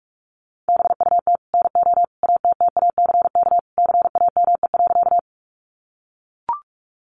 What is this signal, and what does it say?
Morse code, heard as a keyed tone, about 44 wpm, 700 Hz. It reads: BUT NO ATTACK CAME.
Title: Sentences from Top 1500 Words 45wpm